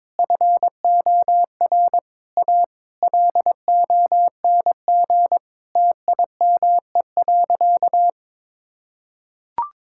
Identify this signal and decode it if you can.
Morse code, heard as a keyed tone, about 22 wpm, 700 Hz. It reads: FOR A LONG TIME.
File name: practice / 22wpm / Sentences with 8-Word Phrases - 22wpm